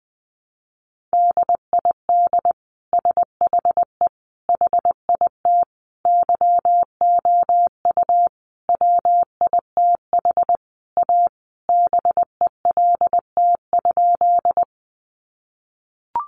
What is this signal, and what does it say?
Morse code, heard as a keyed tone, about 20 wpm, 700 Hz. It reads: DID SHE HIT YOU WITH A BELT?
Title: Sentences from Top 1400 Words 20wpm